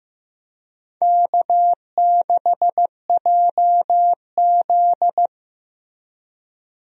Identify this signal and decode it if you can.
Morse code, heard as a keyed tone, about 15 wpm, 700 Hz. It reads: K6JZ
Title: USA Call Signs - 4 Characters - Repeated 2x - 15wpm